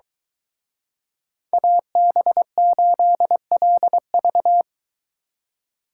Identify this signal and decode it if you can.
Morse code, heard as a keyed tone, about 23 wpm, 700 Hz. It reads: AB8LV